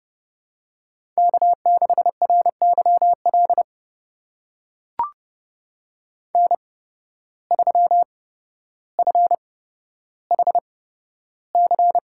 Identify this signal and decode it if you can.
Morse code, heard as a keyed tone, about 30 wpm, spaced as for 13 wpm, 700 Hz. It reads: K6RYL N3FHC
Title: USA Call Signs - 5 Characters - 8x Character Spacing - 30wpm